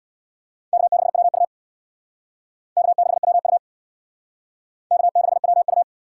Morse code, heard as a keyed tone, about 61 wpm, 700 Hz. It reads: K6WU K6WU K6WU
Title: Speed Racing - USA 1x2 Call Signs 50wpm